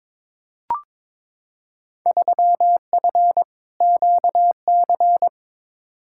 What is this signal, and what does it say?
Morse code, heard as a keyed tone, about 22 wpm, 700 Hz. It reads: 3F QC
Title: Field Day Exchanges - 22wpm